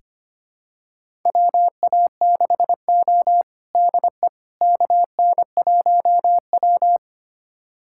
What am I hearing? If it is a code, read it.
Morse code, heard as a keyed tone, about 25 wpm, 700 Hz. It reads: WA6O DE KN1W
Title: USA Callsign DE Callsign - 4 Characters - 25wpm